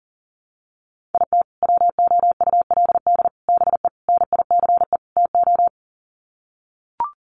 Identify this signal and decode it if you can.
Morse code, heard as a keyed tone, about 40 wpm, 700 Hz. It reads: IT WOULD BE NICE TO
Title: Sets of 5 Words Encore2 40wpm